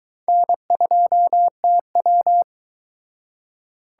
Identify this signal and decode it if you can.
Morse code, heard as a keyed tone, about 23 wpm, 700 Hz. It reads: N2TW